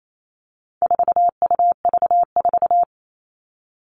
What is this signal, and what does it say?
Morse code, heard as a keyed tone, about 28 wpm, 700 Hz. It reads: 4UV4
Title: All Cut Numbers - 4-Digit 28wpm